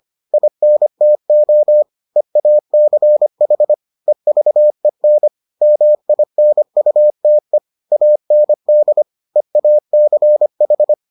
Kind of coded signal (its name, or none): Morse code